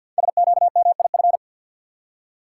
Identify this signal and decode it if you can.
Morse code, heard as a keyed tone, about 48 wpm, 700 Hz. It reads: SQMIF
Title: Code Groups - 5 Letters - 50wpm